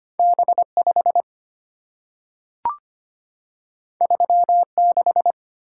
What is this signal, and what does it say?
Morse code, heard as a keyed tone, about 25 wpm, 700 Hz. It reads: B5 36
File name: Two Letter-Number Combinations - 25wpm